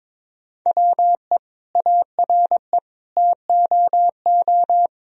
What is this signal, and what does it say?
Morse code, heard as a keyed tone, about 22 wpm, 700 Hz. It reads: WE ARE TOO
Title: Sentences from Top 1000 Words - Review Part 4 - 22wpm